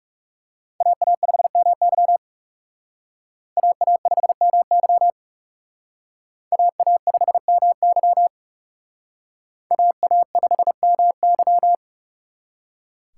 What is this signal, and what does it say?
Morse code, heard as a keyed tone, about 38 wpm, 700 Hz. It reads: AA5MY AA5MY AA5MY AA5MY
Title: Speed Racing - USA 2x2 Call Signs 30wpm